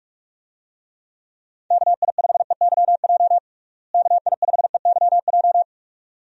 Morse code, heard as a keyed tone, about 46 wpm, 700 Hz. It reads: KI5EYJ KI5EYJ
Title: USA Call Signs - 6 Characters - Repeated 2x - 45wpm